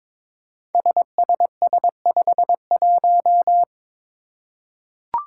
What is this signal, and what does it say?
Morse code, heard as a keyed tone, about 22 wpm, 700 Hz. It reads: SSS51
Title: Five Challenging Letter-Number Combinations 22wpm